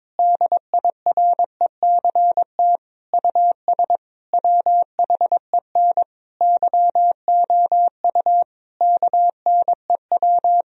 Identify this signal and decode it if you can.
Morse code, heard as a keyed tone, about 22 wpm, 700 Hz. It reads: DIRECT US WHEN YOU KNEW
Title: Sentences from Top 1000 Words - Review Part 2 - 22wpm